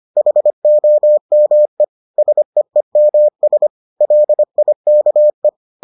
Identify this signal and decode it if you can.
Morse code, heard as a keyed tone, about 25 wpm, 600 Hz. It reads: HOME SEEMS LIKE